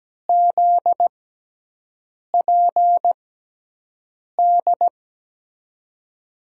Morse code, heard as a keyed tone, about 17 wpm, spaced as for 6 wpm, 700 Hz. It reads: ZPD
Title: USA Call Signs - 6 Characters - 6x Character Spacing - 17wpm